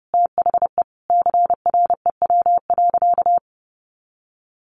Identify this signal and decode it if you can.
Morse code, heard as a keyed tone, about 30 wpm, 700 Hz. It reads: THE CREW.
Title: Sentences from Top 1500 Words 30wpm